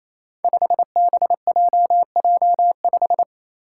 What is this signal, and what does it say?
Morse code, heard as a keyed tone, about 28 wpm, 700 Hz. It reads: 5BJJ5
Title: Five Challenging Letter-Number Combinations 28wpm